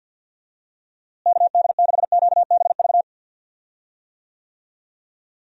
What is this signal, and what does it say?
Morse code, heard as a keyed tone, about 50 wpm, 700 Hz. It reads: KD6QBV
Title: USA Call Signs - 6 Characters - Repeated 2x - 50wpm